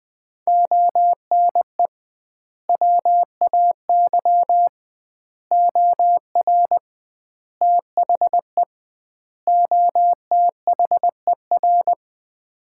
Morse code, heard as a keyed tone, about 21 wpm, 700 Hz. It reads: ONE WAY OR THE OTHER
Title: 2x Word-Spacing - Sets of 5 Words - 20wpm